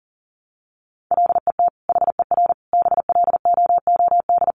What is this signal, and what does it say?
Morse code, heard as a keyed tone, about 40 wpm, 700 Hz. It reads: LET HER BLOOD
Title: Sentences from Top 1200 Words 40wpm